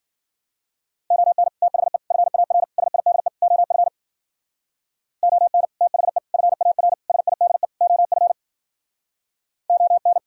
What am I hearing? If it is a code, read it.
Morse code, heard as a keyed tone, about 50 wpm, 700 Hz. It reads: ON THE FAR SIDE OF ON THE FAR SIDE OF ON
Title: Speed-Racing - Sets of 5 Words 40wpm